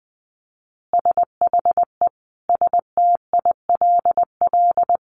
Morse code, heard as a keyed tone, about 20 wpm, 700 Hz. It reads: SHE STILL
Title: Additional Easy Phrases 20wpm